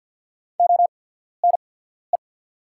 Morse code, heard as a keyed tone, about 50 wpm, spaced as for 14 wpm, 690 Hz. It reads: ONE
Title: 8x Character Spacing - Top 100 Words - 50wpm